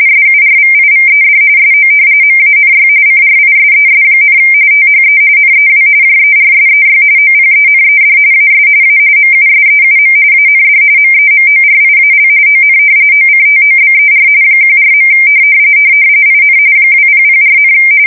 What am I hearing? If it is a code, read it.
RTTY, about 45.45 baud. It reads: SURVIVED SO MANY CRISES, THE NEW JERSEY OPENED AS A MUSEUM AND MEMORIAL ON OCTOBER 14. IN 2004,
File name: fsk170 / battleshipnj